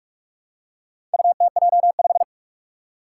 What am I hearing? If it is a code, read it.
Morse code, heard as a keyed tone, about 44 wpm, 700 Hz. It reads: UTJ5
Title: Four Challenging Letter-Number Combinations 45wpm